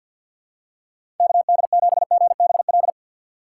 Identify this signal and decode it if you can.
Morse code, heard as a keyed tone, about 52 wpm, 700 Hz. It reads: KD7GBL
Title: USA Call Signs - 6 Characters - 3x Character Spacing - 50wpm